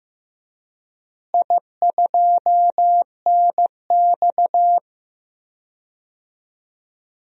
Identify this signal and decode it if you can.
Morse code, heard as a keyed tone, about 15 wpm, 700 Hz. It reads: I2NX
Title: DXCC-100 Call Signs - 4 Characters - Repeated 2x - 15wpm